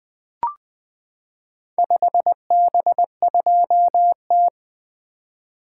Morse code, heard as a keyed tone, about 20 wpm, 700 Hz. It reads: 5B2T
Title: Four Letter-Number Combinations - 20wpm